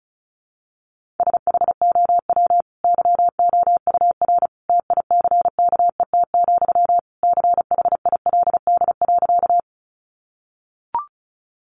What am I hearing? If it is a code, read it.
Morse code, heard as a keyed tone, about 34 wpm, 700 Hz. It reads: SHOW YOUR TICKET, CHILD.
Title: Sentences from Top 1400 Words 35wpm